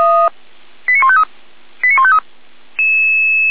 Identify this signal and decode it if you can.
Five-tone selective call (ZVEI), heard as quick runs of five tones, tones of about 70 ms. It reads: 89153 89153